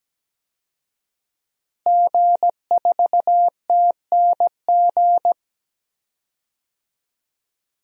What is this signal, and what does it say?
Morse code, heard as a keyed tone, about 17 wpm, 700 Hz. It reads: G4TNG
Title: DXCC-100 Call Signs - 5 Characters - Repeated 2x - 17wpm